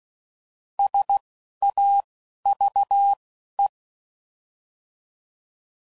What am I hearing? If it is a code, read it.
Morse code, heard as a keyed tone, about 16 wpm, 800 Hz. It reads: S A V E